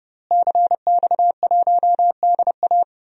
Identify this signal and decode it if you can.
Morse code, heard as a keyed tone, about 31 wpm, 700 Hz. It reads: CX1DA
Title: DXCC-100 Call Signs - 5 Characters - Repeated 3x - 30wpm